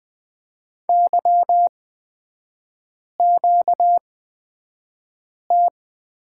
Morse code, heard as a keyed tone, about 20 wpm, spaced as for 5 wpm, 700 Hz. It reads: YQT
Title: 236 Lesson 38e - Letter q - Three Characters 20wpm